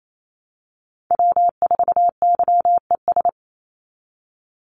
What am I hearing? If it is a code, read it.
Morse code, heard as a keyed tone, about 28 wpm, 700 Hz. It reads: W4YES
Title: Call Signs Repeated 3x - 28wpm